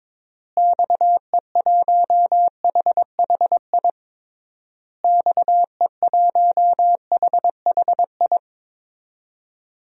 Morse code, heard as a keyed tone, about 22 wpm, 700 Hz. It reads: XE1HHI XE1HHI
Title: DXCC-100 Call Signs - 6 Characters - Repeated 3x - 22wpm